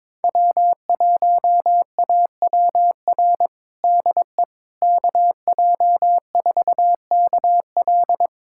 Morse code, heard as a keyed tone, about 22 wpm, 700 Hz. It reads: W1AWR DE KJ4KL
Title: USA Callsign DE Callsign - 5 Characters - 22wpm